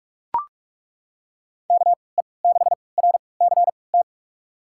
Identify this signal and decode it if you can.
Morse code, heard as a keyed tone, about 45 wpm, spaced as for 25 wpm, 700 Hz. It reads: KE6RCT
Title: USA Call Signs - 6 Characters - 3x Character Spacing - 45wpm